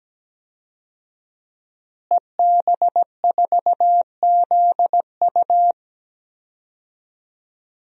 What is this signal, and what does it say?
Morse code, heard as a keyed tone, about 17 wpm, 700 Hz. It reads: EB4ZU
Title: DXCC-100 Call Signs - 5 Characters - Repeated 2x - 17wpm